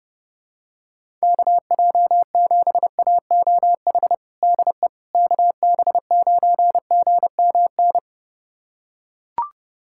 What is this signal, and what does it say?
Morse code, heard as a keyed tone, about 30 wpm, 700 Hz. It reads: KJ7AOH DE KB9GMN
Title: USA Callsign DE Callsign - 6 Characters - 30wpm